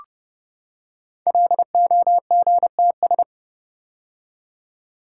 Morse code, heard as a keyed tone, about 30 wpm, 700 Hz. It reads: LOGTS
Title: Code Groups - 5 Letter:Numbers - 30wpm